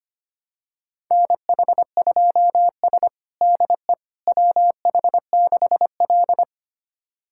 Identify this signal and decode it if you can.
Morse code, heard as a keyed tone, about 25 wpm, 700 Hz. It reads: NH2S DE WH6L
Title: USA Callsign DE Callsign - 4 Characters - 25wpm